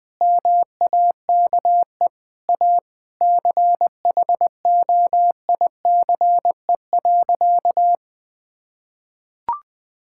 Morse code, heard as a keyed tone, about 21 wpm, 700 Hz. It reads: MAKE A CHOICE.
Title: Sentences from Top 1000 Words - Review Part 6 - 20wpm